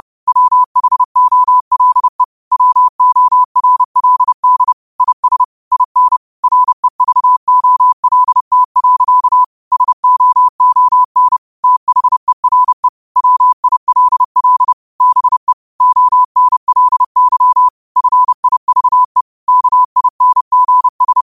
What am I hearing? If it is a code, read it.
Morse code, heard as a keyed tone, about 30 wpm, 1000 Hz. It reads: WHOLE WORLD IS IN REVOLT. SOON THERE WILL BE ONLY FIVE KINGS
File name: qotdfiles / 30wpm